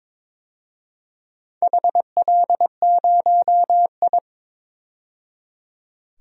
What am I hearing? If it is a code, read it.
Morse code, heard as a keyed tone, about 22 wpm, 690 Hz. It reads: HL0I
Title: DXCC-100 Call Signs - 4 Characters - 22wpm